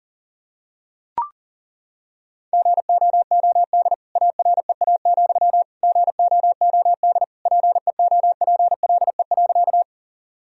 Morse code, heard as a keyed tone, about 40 wpm, 700 Hz. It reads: GOOD AREA, GOOD PEOPLE.
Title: Sentences from Top 1000 Words - Review Part 2 - 40wpm